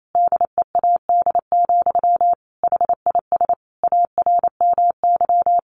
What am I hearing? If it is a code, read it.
Morse code, heard as a keyed tone, about 28 wpm, 700 Hz. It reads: DEAD, HIS ARMY
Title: Sentences from Top 1500 Words 28wpm